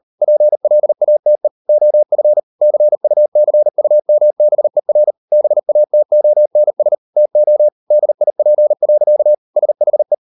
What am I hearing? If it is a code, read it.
Morse code, heard as a keyed tone, about 38 wpm, 600 Hz. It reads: PLATE OF CUCUMBER BATONS TO DIP. SHE